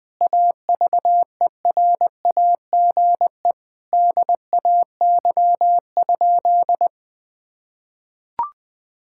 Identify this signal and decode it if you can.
Morse code, heard as a keyed tone, about 19 wpm, 700 Hz. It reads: AVERAGE DAY?
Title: Sentences from Top 1000 Words - Review Part 6 - 20wpm